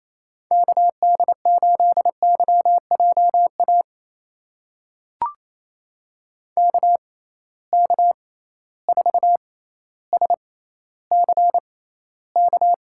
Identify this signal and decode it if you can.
Morse code, heard as a keyed tone, about 28 wpm, spaced as for 14 wpm, 700 Hz. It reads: KD8YJA KK4SCK
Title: USA Call Signs - 6 Characters - 6x Character Spacing - 28wpm